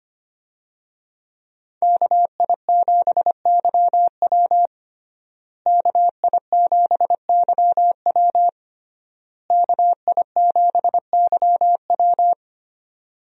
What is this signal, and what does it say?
Morse code, heard as a keyed tone, about 25 wpm, 700 Hz. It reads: KI7YW KI7YW KI7YW
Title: USA Call Signs - 5 Characters - Repeated 3x - 25wpm